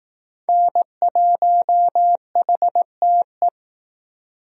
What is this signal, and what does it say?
Morse code, heard as a keyed tone, about 20 wpm, 700 Hz. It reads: N1HTE